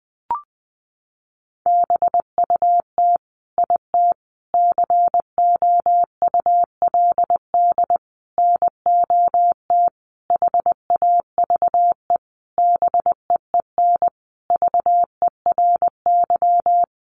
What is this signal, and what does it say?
Morse code, heard as a keyed tone, about 19 wpm, 700 Hz. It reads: BUT IT COULD NOT HAVE BEEN VERY
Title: Sentences from Top 1400 Words 20wpm